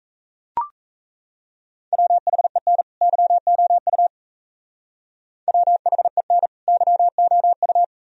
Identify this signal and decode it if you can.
Morse code, heard as a keyed tone, about 34 wpm, 700 Hz. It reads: WHEN YOU WHEN YOU
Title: Speed-Racing - Sets of 2 Words 28wpm